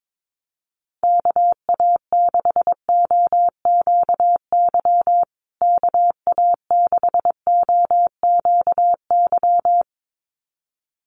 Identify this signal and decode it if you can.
Morse code, heard as a keyed tone, about 22 wpm, 700 Hz. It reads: KA6OQY KA6OQY
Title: USA Call Signs Repeated 2x 22wpm